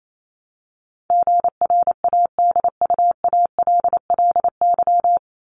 Morse code, heard as a keyed tone, about 28 wpm, 700 Hz. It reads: GRADUALLY
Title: LY Suffix - Top 100 Words with Base Words 28wpm